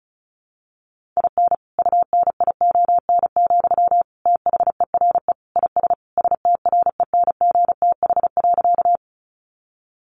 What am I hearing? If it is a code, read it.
Morse code, heard as a keyed tone, about 34 wpm, 700 Hz. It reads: IN UNION, THERE IS STRENGTH.
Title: Sentences from Top 1400 Words 35wpm